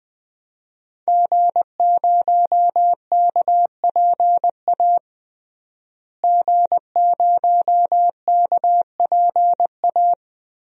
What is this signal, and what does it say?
Morse code, heard as a keyed tone, about 21 wpm, 700 Hz. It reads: G0KPA G0KPA
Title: DXCC-100 Call Signs - 5 Characters - Repeated 2x - 20wpm